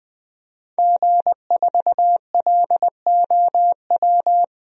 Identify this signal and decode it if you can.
Morse code, heard as a keyed tone, about 20 wpm, 700 Hz. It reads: G4LOW
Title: DXCC-100 Call Signs - 5 Characters - Repeated 2x - 20wpm